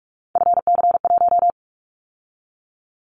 Morse code, heard as a keyed tone, about 46 wpm, 700 Hz. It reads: FC1
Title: Callsign Prefixes for Top 50 DXCC - 45wpm